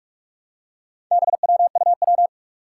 Morse code, heard as a keyed tone, about 43 wpm, 700 Hz. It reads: BWUW
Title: Four Challenging Letter-Number Combinations 45wpm